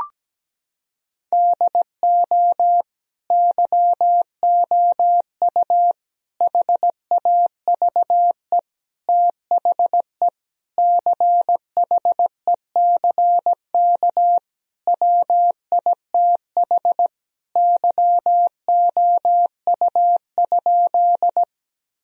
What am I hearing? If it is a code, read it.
Morse code, heard as a keyed tone, about 17 wpm, 700 Hz. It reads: DO YOU HAVE THE CHECK WITH YOU?
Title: Sentences from Top 1000 Words - Review Part 3 - 17wpm